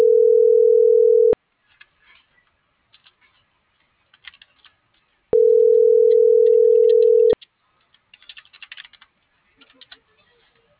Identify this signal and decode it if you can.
Ringback tone (North America), 440 + 480 Hz.